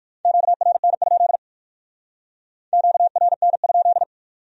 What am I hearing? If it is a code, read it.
Morse code, heard as a keyed tone, about 58 wpm, 700 Hz. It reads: QRN? QRN?